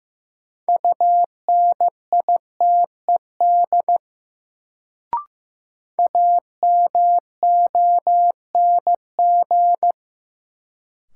Morse code, heard as a keyed tone, about 15 wpm, 700 Hz. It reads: UNITED AMONG
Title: Sets of 2 Words - New Word Drill - 15wpm